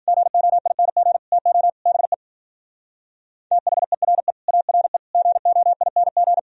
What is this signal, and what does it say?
Morse code, heard as a keyed tone, about 50 wpm, 700 Hz. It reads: GOING TO BE THERE ARE GOING
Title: Speed-Racing - Sets of 5 Words 40wpm